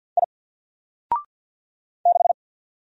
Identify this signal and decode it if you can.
Morse code, heard as a keyed tone, about 48 wpm, 700 Hz. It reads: I 6
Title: Single Character (Letter:Number:Punctuation:Prosign) 50wpm